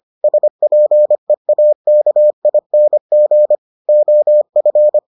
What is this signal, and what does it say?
Morse code, heard as a keyed tone, about 25 wpm, 600 Hz. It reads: SPEAKING OF